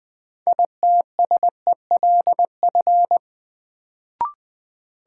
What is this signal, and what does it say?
Morse code, heard as a keyed tone, about 20 wpm, 700 Hz. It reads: ITSELF
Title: Sentences from Top 800 Words - New Word Drill - 20wpm